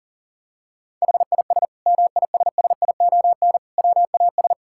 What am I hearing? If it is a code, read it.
Morse code, heard as a keyed tone, about 37 wpm, 700 Hz. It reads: HIS MISSION WAS
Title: Sentences from Top 1600 Words 40wpm